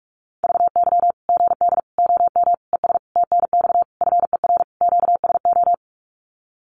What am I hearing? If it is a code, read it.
Morse code, heard as a keyed tone, about 44 wpm, 700 Hz. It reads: VY GD OM ES TNX FER QSO
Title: Common QSO Phrases 45wpm